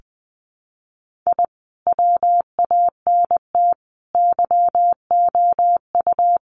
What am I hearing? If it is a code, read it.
Morse code, heard as a keyed tone, about 20 wpm, 700 Hz. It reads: I WANT YOU
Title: Sets of 4 Words 20wpm